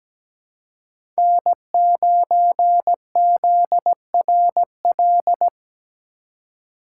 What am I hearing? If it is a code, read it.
Morse code, heard as a keyed tone, about 17 wpm, 700 Hz. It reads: N9ZRL
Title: USA Call Signs - 5 Characters - 2x Character Spacing - 17wpm